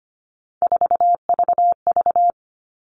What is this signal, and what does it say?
Morse code, heard as a keyed tone, about 25 wpm, 700 Hz. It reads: 4VV
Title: All Cut Numbers - 3-Digit 25wpm